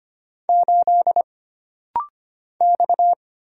Morse code, heard as a keyed tone, about 25 wpm, 700 Hz. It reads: 8 X
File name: Single Character (Letter:Number:Punctuation:Prosign) - Rapid-Fire - 25wpm